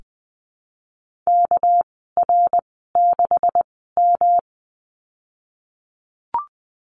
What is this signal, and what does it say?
Morse code, heard as a keyed tone, about 20 wpm, spaced as for 15 wpm, 700 Hz. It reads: KR6M
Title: 213 Lesson 34g - Number 8 - Call Signs 20wpm